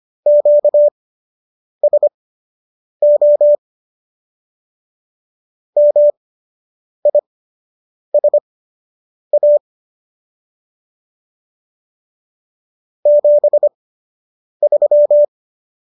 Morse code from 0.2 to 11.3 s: QSO MISA

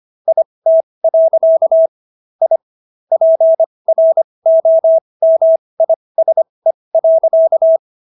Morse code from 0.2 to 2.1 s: IT.

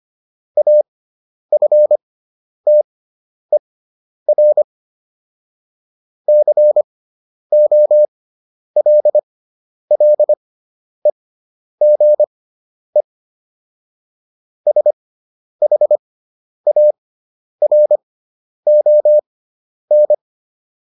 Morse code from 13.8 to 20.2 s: SHARON

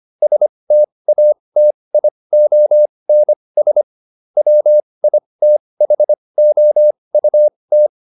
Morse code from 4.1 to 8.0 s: WITHOUT

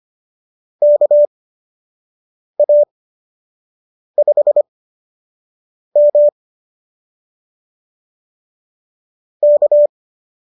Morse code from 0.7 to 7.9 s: KA5M

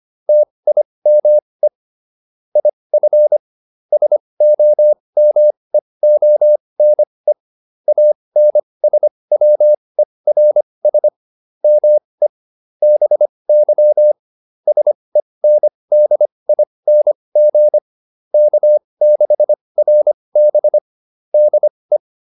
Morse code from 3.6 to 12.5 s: SOMEONE ANSWERS ME

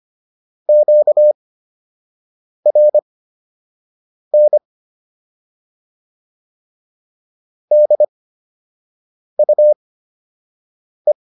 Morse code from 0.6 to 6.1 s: QRN

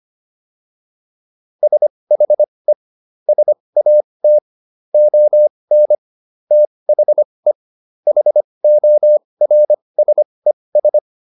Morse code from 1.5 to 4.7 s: SHE SAT

